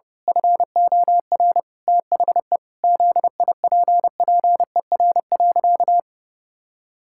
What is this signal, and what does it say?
Morse code, heard as a keyed tone, about 30 wpm, 700 Hz. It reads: FOR THE ZIPPER.
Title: Pangram Challenge v2 - 30wpm